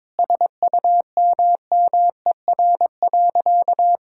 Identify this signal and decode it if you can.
Morse code, heard as a keyed tone, about 22 wpm, 700 Hz. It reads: SUMMER.
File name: Sentences with 9-Word Phrases - 22wpm